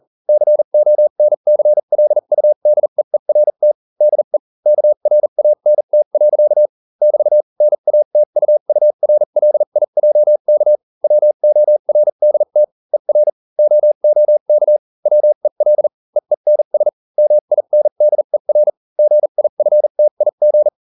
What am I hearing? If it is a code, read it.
Morse code, heard as a keyed tone, about 40 wpm, spaced as for 35 wpm, 600 Hz. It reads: CONCLUDEERT DE KRANT. <BT> NATUURLIJK WORDT ER OOK WEL EENS MINDER GIFTIG